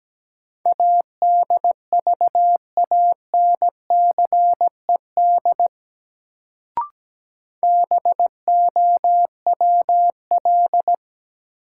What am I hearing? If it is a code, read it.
Morse code, heard as a keyed tone, about 17 wpm, 700 Hz. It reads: ADVANCED BOWL
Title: Sentences from Top 1200 Words - New Word Drill Alt - 17wpm